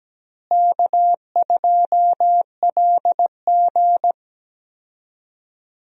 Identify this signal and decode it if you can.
Morse code, heard as a keyed tone, about 17 wpm, 700 Hz. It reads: K2LG
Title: USA Call Signs - 4 Characters - Repeated 2x - 17wpm